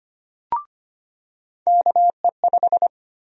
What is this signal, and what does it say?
Morse code, heard as a keyed tone, about 25 wpm, 700 Hz. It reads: KE5
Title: USA Call Sign Prefixes - 25wpm